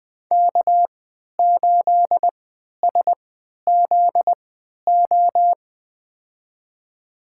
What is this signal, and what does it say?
Morse code, heard as a keyed tone, about 20 wpm, spaced as for 11 wpm, 700 Hz. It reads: K8SZO
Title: USA Call Signs - 5 Characters - 3x Character Spacing - 20wpm